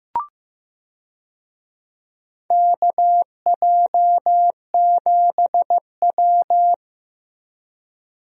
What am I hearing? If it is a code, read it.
Morse code, heard as a keyed tone, about 15 wpm, 700 Hz. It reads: KJ7W
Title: USA Call Signs - 4 Characters - Repeated 2x - 15wpm